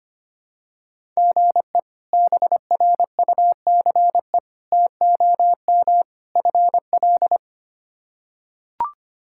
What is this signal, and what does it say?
Morse code, heard as a keyed tone, about 25 wpm, 700 Hz. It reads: GE BRUCE TOM FL
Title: K1USN SST QSO Practice - 25wpm